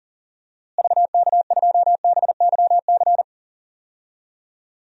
Morse code, heard as a keyed tone, about 38 wpm, 700 Hz. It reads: VK2BYC
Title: DXCC-100 Call Signs - 6 Characters - Repeated 3x - 40wpm